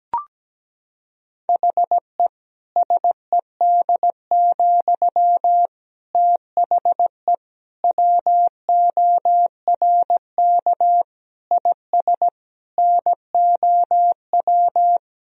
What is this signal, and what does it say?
Morse code, heard as a keyed tone, about 17 wpm, 700 Hz. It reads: HE SED, THE WORK IS NOW
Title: Sentences from Top 400 Words and Abbreviations 17wpm